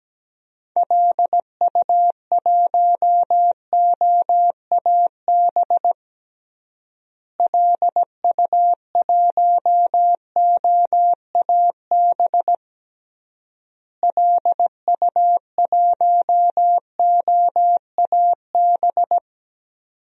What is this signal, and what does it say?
Morse code, heard as a keyed tone, about 17 wpm, 700 Hz. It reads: LU1OAB LU1OAB LU1OAB
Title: DXCC-100 Call Signs - 6 Characters - Repeated 3x - 17wpm